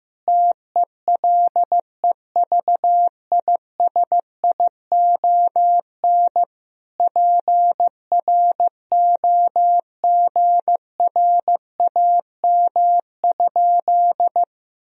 Morse code, heard as a keyed tone, about 15 wpm, 700 Hz. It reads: TELEVISION PROGRAM?
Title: Sentences from Top 1000 Words - Review Part 4 - 15wpm